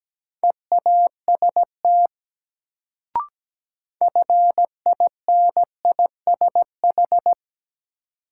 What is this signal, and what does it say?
Morse code, heard as a keyed tone, about 17 wpm, 700 Hz. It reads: EAST FINISH
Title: Sentences from Top 1000 Words - Review Part 3 - New Word Drill - 17wpm